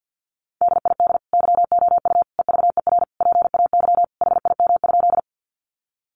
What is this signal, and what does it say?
Morse code, heard as a keyed tone, about 48 wpm, 700 Hz. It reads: DID YOU EVER PAY HIM?
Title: Sentences from Top 700 Words 50wpm